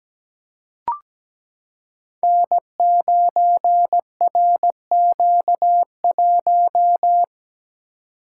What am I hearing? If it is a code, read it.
Morse code, heard as a keyed tone, about 17 wpm, 700 Hz. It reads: N9RQ1
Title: Code Groups - 5 Letter:Numbers - 17wpm